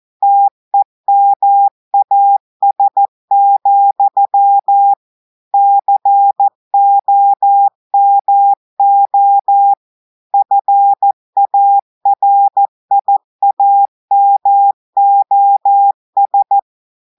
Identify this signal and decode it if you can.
Morse code, heard as a keyed tone, about 14 wpm, 800 Hz. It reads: TEMAS, COMO FARIAMOS
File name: audios / portugais / portugais texte en clair 1 vitesse 14 mots minutes